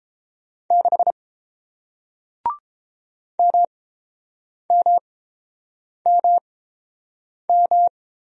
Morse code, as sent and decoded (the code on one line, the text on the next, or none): -.... / -- -- -- --
6 MMMM